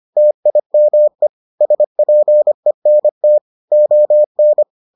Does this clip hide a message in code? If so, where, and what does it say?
Yes: Morse code from 0.2 to 4.6 s, heard as a keyed tone, about 25 wpm, 600 Hz. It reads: TIME SPENT ON